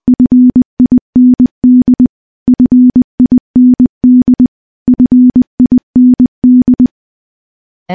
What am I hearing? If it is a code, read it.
Morse code, heard as a keyed tone, about 20 wpm, 250 Hz. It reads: FIND FIND FIND
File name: morse-training-20WPM